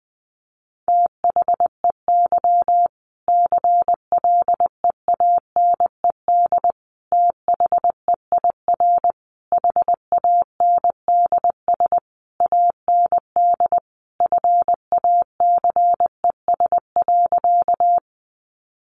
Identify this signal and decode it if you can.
Morse code, heard as a keyed tone, about 20 wpm, 700 Hz. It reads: THEY CLEANED THEIR HANDS AND FACES.
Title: A Sound Of Thunder 20wpm